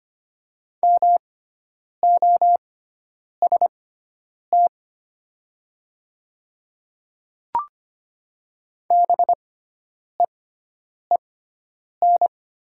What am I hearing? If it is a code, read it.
Morse code, heard as a keyed tone, about 25 wpm, spaced as for 9 wpm, 700 Hz. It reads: MOST BEEN
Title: 6x Character Spacing - Top 100 Words - 25wpm